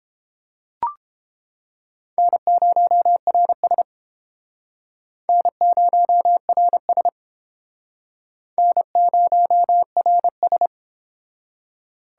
Morse code, heard as a keyed tone, about 30 wpm, 700 Hz. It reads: N0RS N0RS N0RS